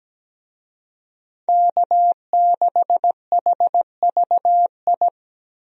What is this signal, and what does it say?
Morse code, heard as a keyed tone, about 17 wpm, 700 Hz. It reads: K6HVI
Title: USA Call Signs - 5 Characters - 17wpm